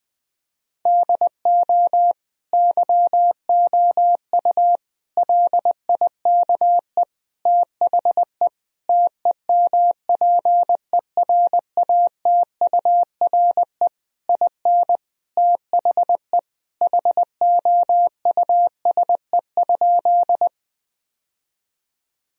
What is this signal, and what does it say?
Morse code, heard as a keyed tone, about 20 wpm, 700 Hz. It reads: DO YOU LIKE THE TEMPERATURE IN THE HOUSE?
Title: Sentences from Top 1000 Words - Review Part 4 - 20wpm